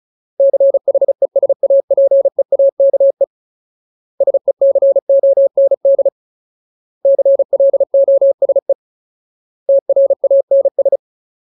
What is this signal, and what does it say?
Morse code, heard as a keyed tone, about 34 wpm, 550 Hz. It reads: CHESAPEAKE SECOND CLOSE TRANS
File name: Ultimate Word List / 35WPM Ultimate